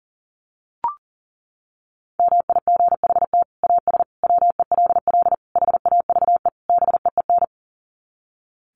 Morse code, heard as a keyed tone, about 40 wpm, 700 Hz. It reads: MIGHT AS WELL HAVE BEEN
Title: Sets of 5 Words Encore2 40wpm